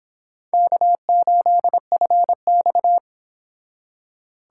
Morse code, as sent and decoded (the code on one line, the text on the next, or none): -.- ---.. ..-. -..-
K8FX